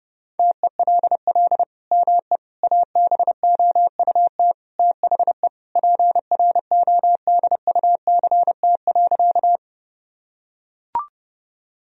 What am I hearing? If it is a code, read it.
Morse code, heard as a keyed tone, about 30 wpm, 700 Hz. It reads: TELL ME ABOUT THE PRODUCT.
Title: Sentences from Top 1000 Words - Review Part 2 - 30wpm